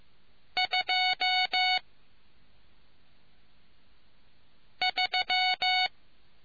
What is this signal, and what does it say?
Morse code, heard as a keyed tone, about 15 wpm, 740 Hz. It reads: 2 3